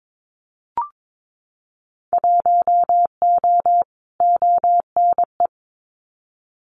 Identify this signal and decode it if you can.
Morse code, heard as a keyed tone, about 22 wpm, 700 Hz. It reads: 1O ONE
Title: Winter Field Day Exchanges 22wpm